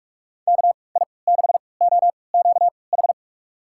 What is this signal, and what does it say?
Morse code, heard as a keyed tone, about 42 wpm, spaced as for 25 wpm, 700 Hz. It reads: KI6OQH
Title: USA Call Signs - 6 Characters - 3x Character Spacing - 45wpm